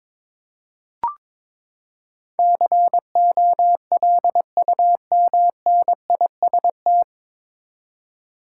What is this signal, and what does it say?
Morse code, heard as a keyed tone, about 22 wpm, 700 Hz. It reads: COLUMNIST
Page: Morse Code Ninja practice 22wpm